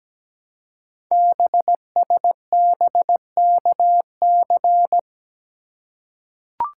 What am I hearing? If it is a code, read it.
Morse code, heard as a keyed tone, about 17 wpm, 700 Hz. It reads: BSBKC
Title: Code Groups - 5 Letters - 17wpm